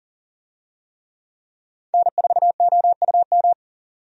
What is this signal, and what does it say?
Morse code, heard as a keyed tone, about 40 wpm, 700 Hz. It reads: N4OUM
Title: USA Call Signs - 5 Characters - 40wpm